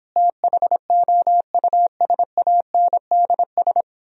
Morse code, heard as a keyed tone, about 29 wpm, 700 Hz. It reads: THOUSANDS